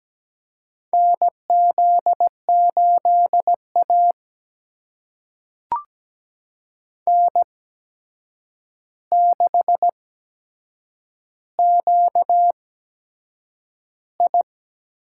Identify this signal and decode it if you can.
Morse code, heard as a keyed tone, about 17 wpm, spaced as for 7 wpm, 700 Hz. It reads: NZ8A N6QI
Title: USA Call Signs - 4 Characters - 8x Character Spacing - 17wpm